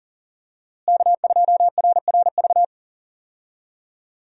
Morse code, heard as a keyed tone, about 40 wpm, 700 Hz. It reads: K2RRV
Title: USA Call Signs - 5 Characters - Repeated 3x - 40wpm